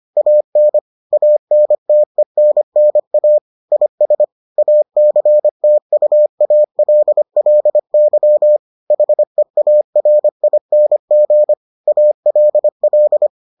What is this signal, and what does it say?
Morse code, heard as a keyed tone, about 25 wpm, 600 Hz. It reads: AN ANTENNA IS ACTUALLY HEARING ALL